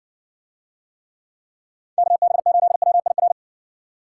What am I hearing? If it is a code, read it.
Morse code, heard as a keyed tone, about 60 wpm, 700 Hz. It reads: KB8WIR